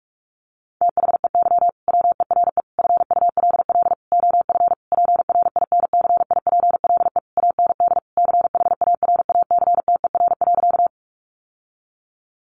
Morse code, heard as a keyed tone, about 44 wpm, 700 Hz. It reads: THEY WERE FULL OF PRINCIPLE AND CHARACTER.
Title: Sentences from Top 1400 Words 45wpm